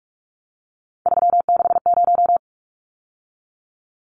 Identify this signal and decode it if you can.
Morse code, heard as a keyed tone, about 44 wpm, 700 Hz. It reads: 360
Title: Numbers - 3-Digit 45wpm